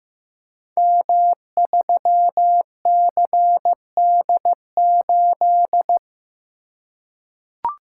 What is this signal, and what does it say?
Morse code, heard as a keyed tone, about 15 wpm, 700 Hz. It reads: M3CD8
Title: Code Groups - 5 Letter:Numbers - 15wpm